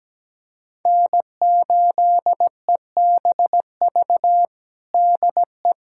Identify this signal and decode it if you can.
Morse code, heard as a keyed tone, about 17 wpm, 700 Hz. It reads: N8EBV DE
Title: USA Callsign DE Callsign - 5 Characters - 17wpm